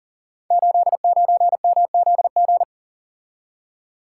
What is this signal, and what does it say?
Morse code, heard as a keyed tone, about 38 wpm, 700 Hz. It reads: 89MZG
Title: Code Groups - 5 Letter:Numbers - 40wpm